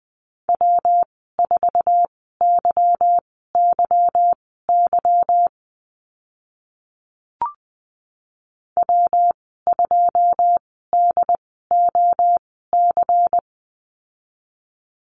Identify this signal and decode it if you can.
Morse code, heard as a keyed tone, about 20 wpm, spaced as for 14 wpm, 700 Hz. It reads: W4YYY W2DOC
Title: 161 Lesson 26g - Letter g - Call Signs 20wpm